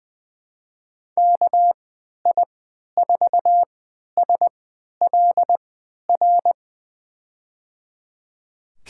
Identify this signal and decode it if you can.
Morse code, heard as a keyed tone, about 21 wpm, spaced as for 12 wpm, 700 Hz. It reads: KI4SLR
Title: USA Call Signs - 6 Characters - 3x Character Spacing - 20wpm